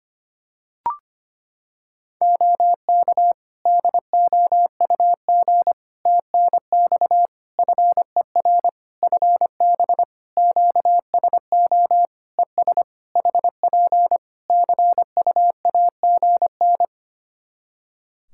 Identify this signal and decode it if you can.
Morse code, heard as a keyed tone, about 25 wpm, 700 Hz. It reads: OK DOUG TNX FER FB QSO ES HP CUAGN
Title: K1USN SST QSO Practice 25wpm